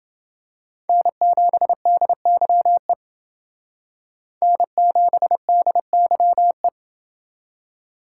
Morse code, heard as a keyed tone, about 29 wpm, 700 Hz. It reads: N7DYE N7DYE